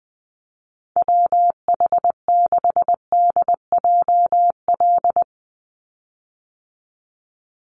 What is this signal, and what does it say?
Morse code, heard as a keyed tone, about 20 wpm, 700 Hz. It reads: WH6DJL
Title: USA Call Signs 20wpm